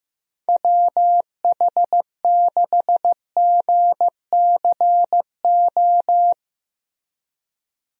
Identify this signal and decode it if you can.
Morse code, heard as a keyed tone, about 15 wpm, 700 Hz. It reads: WH6GCO
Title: USA Call Signs - 6 Characters - Repeated 3x - 15wpm